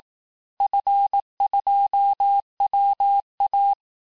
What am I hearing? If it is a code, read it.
Morse code, heard as a keyed tone, about 18 wpm, 800 Hz. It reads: F2WA
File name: cours-du-lundi files / cours-1852-FIN-du-QSO-vitesse-17MPM